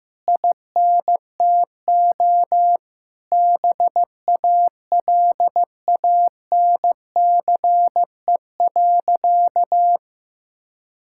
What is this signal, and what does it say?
Morse code, heard as a keyed tone, about 15 wpm, 700 Hz. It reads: INTO BALANCE.
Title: Sentences from Top 1000 Words - Review Part 6 - 15wpm